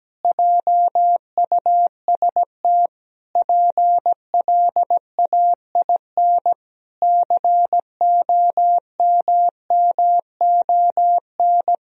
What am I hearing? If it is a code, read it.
Morse code, heard as a keyed tone, about 17 wpm, 700 Hz. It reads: JUST PLAIN COMMON
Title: Sentences from Top 1000 Words - Review Part 4 - 17wpm